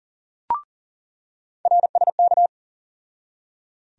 Morse code, heard as a keyed tone, about 40 wpm, 700 Hz. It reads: RSK
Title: USA Call Sign Postfixes - 3 Letters 40wpm